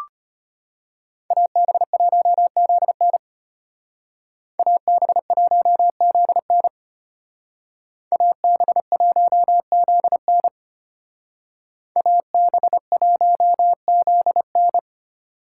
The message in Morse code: AB1ZN AB1ZN AB1ZN AB1ZN

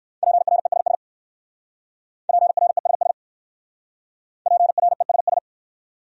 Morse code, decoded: PRESS PRESS PRESS

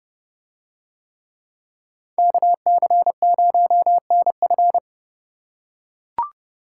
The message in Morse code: KC0NF

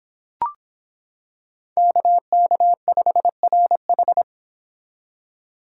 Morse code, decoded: KK5RH

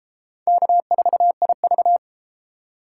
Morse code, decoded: K4IV

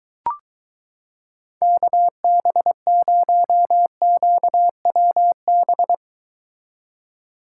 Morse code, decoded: KB0QWB

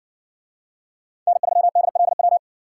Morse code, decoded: N3DLW

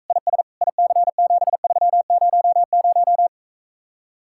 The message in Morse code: IS IC7300

